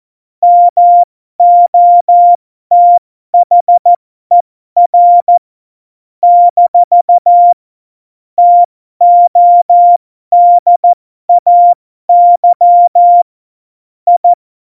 Morse code, heard as a keyed tone, about 14 wpm, spaced as for 12 wpm, 700 Hz. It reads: MOTHER - TODAY I
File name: Morsekurs / Witze